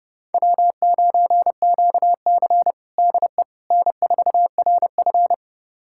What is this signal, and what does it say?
Morse code, heard as a keyed tone, about 30 wpm, 700 Hz. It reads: W9QC DE N4RF
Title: USA Callsign DE Callsign - 4 Characters - 30wpm